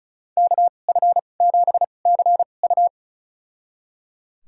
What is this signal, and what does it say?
Morse code, heard as a keyed tone, about 34 wpm, spaced as for 25 wpm, 700 Hz. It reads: KF7CU
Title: USA Call Signs - 2x Character Spacing - 35wpm